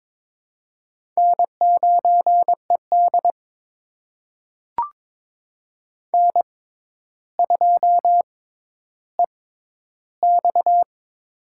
Morse code, heard as a keyed tone, about 22 wpm, spaced as for 11 wpm, 700 Hz. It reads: N9ED N2EX
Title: USA Call Signs - 4 Characters - 6x Character Spacing - 22wpm